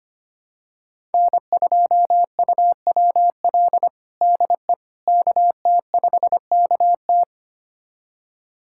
Morse code, heard as a keyed tone, about 25 wpm, 700 Hz. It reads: N2UWL DE KT5KT
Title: USA Callsign DE Callsign - 5 Characters - 25wpm